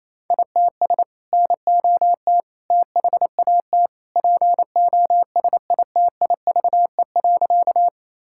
Morse code, heard as a keyed tone, about 28 wpm, 700 Hz. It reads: ITS NOT THAT POSITIVE.
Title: Sentences from Top 1000 Words - Review Part 5 - 28wpm